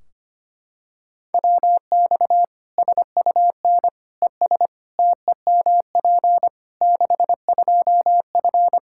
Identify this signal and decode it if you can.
Morse code, heard as a keyed tone, about 25 wpm, 700 Hz. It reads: WX SUN ES TEMP 62F
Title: Practice QSO - Exchange 2 - Info - 25wpm